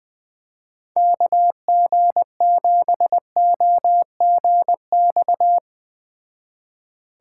Morse code, heard as a keyed tone, about 21 wpm, 700 Hz. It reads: KG7OGX